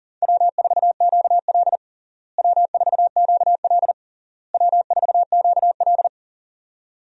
Morse code, heard as a keyed tone, about 37 wpm, 700 Hz. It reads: W4QL W4QL W4QL
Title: USA Call Signs - 4 Characters - Repeated 3x - 40wpm